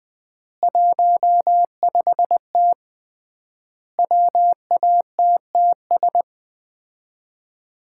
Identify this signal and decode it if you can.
Morse code, heard as a keyed tone, about 20 wpm, 700 Hz. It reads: 15T WATTS
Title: Common QSO Phrases - 3x Word-Spacing - 20wpm